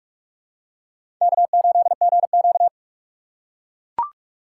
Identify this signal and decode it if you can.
Morse code, heard as a keyed tone, about 44 wpm, 690 Hz. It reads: K8GQ
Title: USA Call Signs - 4 Characters - 3x Character Spacing - 45wpm